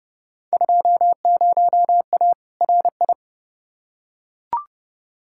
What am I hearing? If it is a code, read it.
Morse code, heard as a keyed tone, about 32 wpm, 700 Hz. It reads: 20A RI
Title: Field Day Exchanges Encore - 30wpm